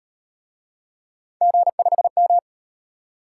Morse code, heard as a keyed tone, about 38 wpm, 700 Hz. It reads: G5M